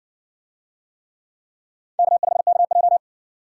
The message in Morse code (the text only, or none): K5KJ